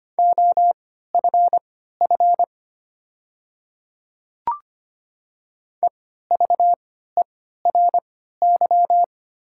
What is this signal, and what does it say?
Morse code, heard as a keyed tone, about 25 wpm, spaced as for 14 wpm, 690 Hz. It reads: OFF EVERY
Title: 3x Character Spacing - Top 200 Words - 25wpm